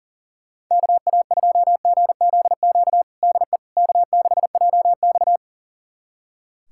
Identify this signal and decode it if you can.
Morse code, heard as a keyed tone, about 40 wpm, 700 Hz. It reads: KA2GZQ DE K6JX
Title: USA Callsign DE Callsign 40wpm